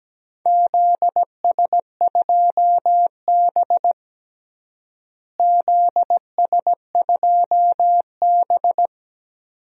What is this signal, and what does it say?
Morse code, heard as a keyed tone, about 17 wpm, 700 Hz. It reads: ZS2B ZS2B
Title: DXCC-100 Call Signs - 4 Characters - Repeated 2x - 17wpm